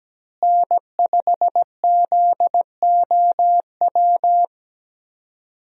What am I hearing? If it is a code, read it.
Morse code, heard as a keyed tone, about 17 wpm, 700 Hz. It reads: N5ZOW
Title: USA Call Signs - 5 Characters - Repeated 2x - 17wpm